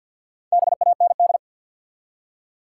Morse code, heard as a keyed tone, about 47 wpm, 700 Hz. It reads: BAND
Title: Four Letter Words - 50wpm